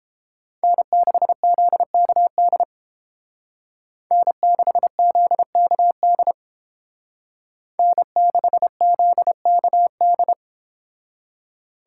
Morse code, heard as a keyed tone, about 30 wpm, 700 Hz. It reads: N6ZKD N6ZKD N6ZKD